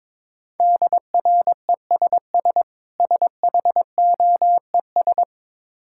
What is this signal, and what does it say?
Morse code, heard as a keyed tone, about 22 wpm, 690 Hz. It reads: DRESS SHOES
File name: Easy Words - Speed-Racing with Hints - 22wpm